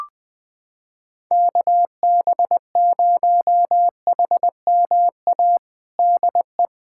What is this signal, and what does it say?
Morse code, heard as a keyed tone, about 20 wpm, 700 Hz. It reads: KB0HMA DE
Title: USA Callsign DE Callsign - 6 Characters - 20wpm